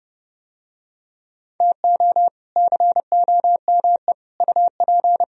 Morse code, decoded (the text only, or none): TO COME UP